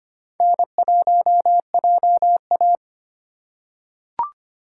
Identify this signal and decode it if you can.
Morse code, heard as a keyed tone, about 25 wpm, 700 Hz. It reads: N1JA
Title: USA Call Signs - 4 Characters - 4x Character Spacing - 25wpm